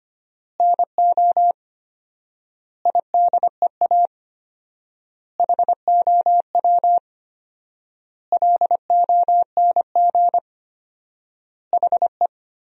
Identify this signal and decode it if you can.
Morse code, heard as a keyed tone, about 25 wpm, 700 Hz. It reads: NO IDEA HOW LONG HE
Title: 4x Word-Spacing - Sentences from Top 300 Words - 25wpm